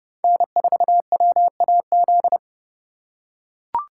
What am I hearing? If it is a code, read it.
Morse code, heard as a keyed tone, about 30 wpm, 700 Hz. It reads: N4WAZ
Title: USA Call Signs - 5 Characters - 3x Character Spacing - 30wpm